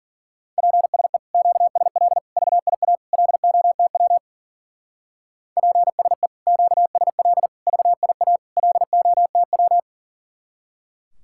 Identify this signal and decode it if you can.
Morse code, heard as a keyed tone, about 49 wpm, 700 Hz. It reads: PSE QSL VIA LOTW PSE QSL VIA LOTW
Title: Speed-Racing Common QSO Phrases Encore - 40wpm